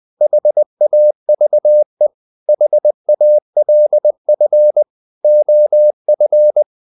Morse code, heard as a keyed tone, about 20 wpm, 600 Hz. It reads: HAVE HALF OF
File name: the old man and the sea part 2 20wpm